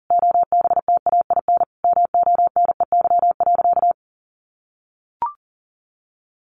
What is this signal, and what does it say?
Morse code, heard as a keyed tone, about 40 wpm, 700 Hz. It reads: OBTAIN MONEY.
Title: Sentences from Top 1300 Words 40wpm